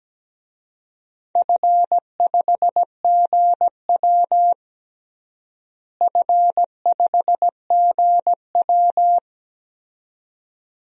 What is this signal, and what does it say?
Morse code, heard as a keyed tone, about 17 wpm, 700 Hz. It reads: F5GW F5GW
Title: DXCC-100 Call Signs - 4 Characters - Repeated 2x - 17wpm